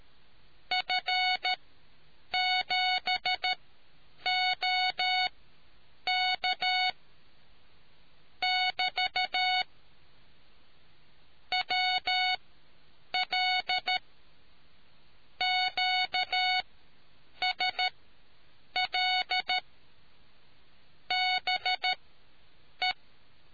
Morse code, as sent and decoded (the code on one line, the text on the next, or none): ..-. --... --- -.- / -...- / .-- .-.. / --.- ... .-.. / -... .
F7OK <BT> WL QSL BE